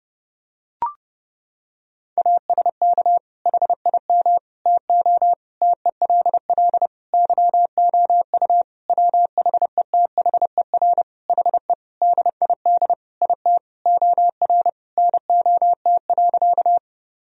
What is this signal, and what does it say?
Morse code, heard as a keyed tone, about 30 wpm, 700 Hz. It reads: ASK HIM TO TELL YOU WHETHER HE DID IT OR NOT.
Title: Sentences from Top 1000 Words - Review Part 3 - 30wpm